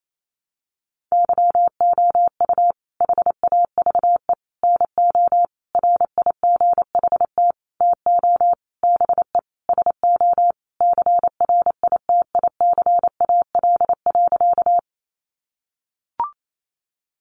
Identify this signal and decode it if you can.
Morse code, heard as a keyed tone, about 28 wpm, 700 Hz. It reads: YOU HAVE NO RIGHT TO BE SO CRITICAL.
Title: Sentences from Top 1100 Words 28wpm